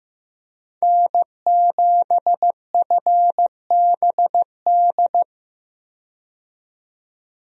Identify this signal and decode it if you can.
Morse code, heard as a keyed tone, about 15 wpm, 700 Hz. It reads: N7FBD
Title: USA Call Signs - 5 Characters - 15wpm